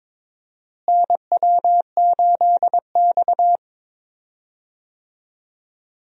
Morse code, heard as a keyed tone, about 22 wpm, 700 Hz. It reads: NW8X
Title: USA Call Signs - 4 Characters - Repeated 3x - 22wpm